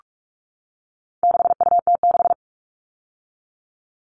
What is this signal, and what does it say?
Morse code, heard as a keyed tone, about 44 wpm, 700 Hz. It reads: 6UT6
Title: All Cut Numbers - 4-Digit 45wpm